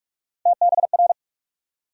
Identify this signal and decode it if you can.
Morse code, heard as a keyed tone, about 46 wpm, 700 Hz. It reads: TBR
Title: USA Call Sign Postfixes - 3 Letters 45wpm